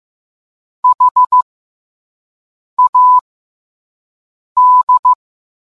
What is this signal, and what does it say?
Morse code, heard as a keyed tone, about 15 wpm, spaced as for 5 wpm, 1000 Hz. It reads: HAD